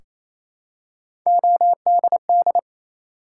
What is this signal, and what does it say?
Morse code, heard as a keyed tone, about 28 wpm, 700 Hz. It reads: ODD
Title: Three Letter Words - 28wpm